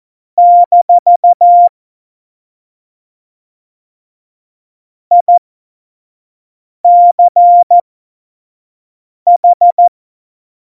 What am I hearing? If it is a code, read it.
Morse code, heard as a keyed tone, about 14 wpm, spaced as for 5 wpm, 700 Hz. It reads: - ICH